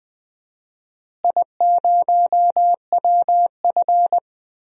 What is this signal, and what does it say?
Morse code, heard as a keyed tone, about 20 wpm, 700 Hz. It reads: I0WF
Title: DXCC-100 Call Signs - 4 Characters - Repeated 2x - 20wpm